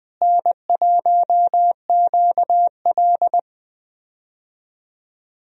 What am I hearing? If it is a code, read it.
Morse code, heard as a keyed tone, about 20 wpm, 700 Hz. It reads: N1QL
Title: USA Call Signs - 4 Characters - Repeated 2x - 20wpm